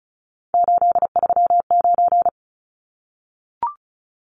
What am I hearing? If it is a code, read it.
Morse code, heard as a keyed tone, about 34 wpm, 700 Hz. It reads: 839
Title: Numbers - 3-Digit 35wpm